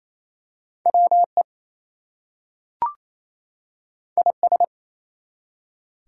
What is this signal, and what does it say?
Morse code, heard as a keyed tone, about 27 wpm, 700 Hz. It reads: WE IS
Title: Two Letter Words - 28wpm